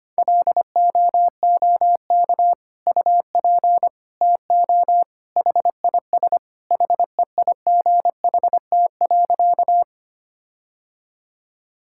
Morse code, heard as a keyed tone, about 25 wpm, 700 Hz. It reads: LOOK UP TO HIS HEIGHT.
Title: Sentences from Top 1600 Words 25wpm